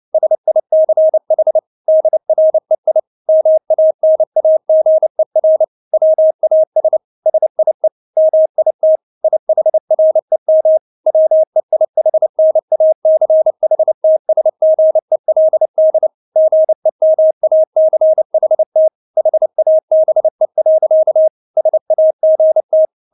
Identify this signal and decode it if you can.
Morse code, heard as a keyed tone, about 29 wpm, 630 Hz. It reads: SICH DREI MANAGER WAS SIE MIT IHREM WEIHNACHTSGELD GEMACHT HABE. SAGT
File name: Witz10 29wpm